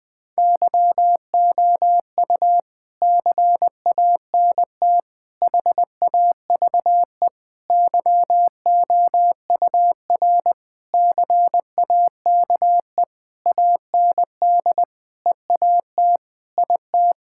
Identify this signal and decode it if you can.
Morse code, heard as a keyed tone, about 20 wpm, 700 Hz. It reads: YOU CANT HAVE YOUR CAKE AND EAT IT
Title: Sentences with 9-Word Phrases - Spoken Beginning - 20wpm